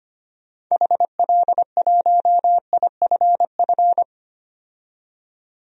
Morse code, heard as a keyed tone, about 25 wpm, 700 Hz. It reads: HL1IFF
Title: DXCC-100 Call Signs - 6 Characters - 25wpm